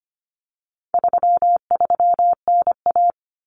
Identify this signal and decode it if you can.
Morse code, heard as a keyed tone, about 25 wpm, 700 Hz. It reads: 33NA
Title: Common Cut Numbers - 4-Digit 25wpm